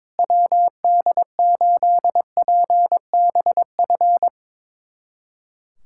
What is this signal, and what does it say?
Morse code, heard as a keyed tone, about 22 wpm, 700 Hz. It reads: WD8PBF
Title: USA Callsign DE Callsign - 6 Characters - 22wpm